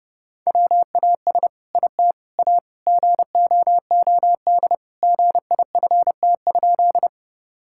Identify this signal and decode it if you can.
Morse code, heard as a keyed tone, about 32 wpm, 700 Hz. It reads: WAS IT A GOOD GIFT?
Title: Sentences from Top 1000 Words - Review Part 6 - 30wpm